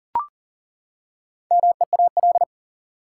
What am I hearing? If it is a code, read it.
Morse code, heard as a keyed tone, about 40 wpm, 700 Hz. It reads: MEAL
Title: Four Letter Words - 40wpm